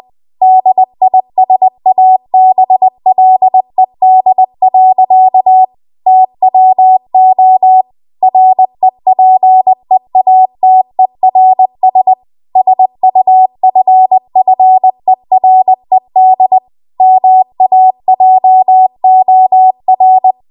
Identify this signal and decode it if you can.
Morse code, heard as a keyed tone, about 20 wpm, 750 Hz. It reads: DISABLED. TWO REPEATERS SUFFERED MAJOR